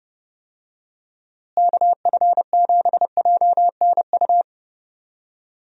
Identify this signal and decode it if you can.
Morse code, heard as a keyed tone, about 30 wpm, 700 Hz. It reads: KF7JNU
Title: USA Call Signs - 6 Characters - Repeated 2x - 30wpm